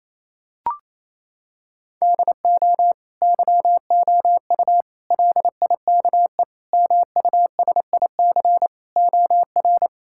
Morse code, heard as a keyed tone, about 28 wpm, 700 Hz. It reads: DO YOU LIKE MUSIC OR
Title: Sentences from Top 1000 Words - Review Part 6 - 28wpm